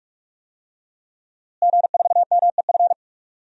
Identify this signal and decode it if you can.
Morse code, heard as a keyed tone, about 46 wpm, 700 Hz. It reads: G4MEF
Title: DXCC-100 Call Signs - 5 Characters - Repeated 2x - 45wpm